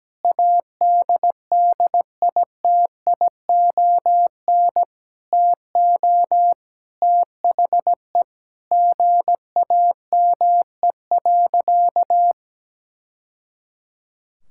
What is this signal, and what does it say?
Morse code, heard as a keyed tone, about 17 wpm, 700 Hz. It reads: ADDITION TO THE GAME.
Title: Sentences from Top 1000 Words - Review Part 6 - 17wpm